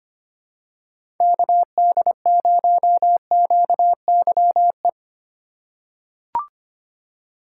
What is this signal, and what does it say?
Morse code, heard as a keyed tone, about 25 wpm, 700 Hz. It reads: KD0QYE
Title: USA Call Signs - 6 Characters - 2x Character Spacing - 25wpm